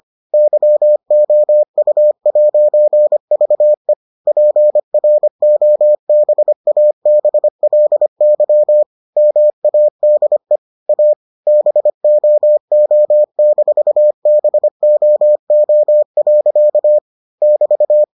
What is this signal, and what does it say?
Morse code, heard as a keyed tone, about 25 wpm, 600 Hz. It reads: YOU'VE PROBABLY MADE A BOO-BOO. <BT>